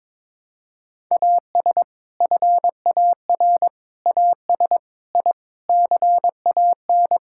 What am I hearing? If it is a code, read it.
Morse code, heard as a keyed tone, about 21 wpm, 700 Hz. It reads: AS FAR AS I CAN
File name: Sentences with 7-Word Phrases - 22wpm